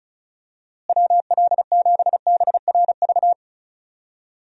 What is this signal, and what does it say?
Morse code, heard as a keyed tone, about 36 wpm, 700 Hz. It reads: WL7BRV
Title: USA Call Signs - 6 Characters - Repeated 2x - 35wpm